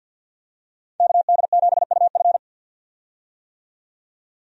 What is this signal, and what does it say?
Morse code, heard as a keyed tone, about 52 wpm, 700 Hz. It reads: KD7UF
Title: USA Call Signs - 5 Characters - 50wpm